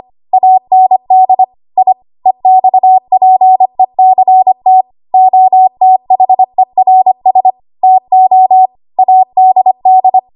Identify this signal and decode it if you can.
Morse code, heard as a keyed tone, about 25 wpm, 750 Hz. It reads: AND I EXPECT OTHERS TO ADD
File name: Archive / 25 WPM / 25WPM